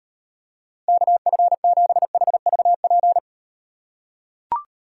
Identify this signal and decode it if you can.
Morse code, heard as a keyed tone, about 38 wpm, 700 Hz. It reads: KF7HVP